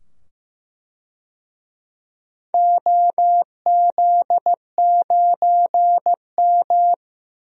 Morse code, heard as a keyed tone, about 15 wpm, 700 Hz. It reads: OZ9M
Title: DXCC-100 Call Signs - 4 Characters - 15wpm